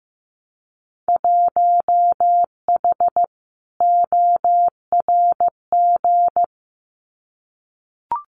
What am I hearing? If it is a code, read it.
Morse code, heard as a keyed tone, about 15 wpm, 700 Hz. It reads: 1H ORG
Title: Winter Field Day Exchanges 15wpm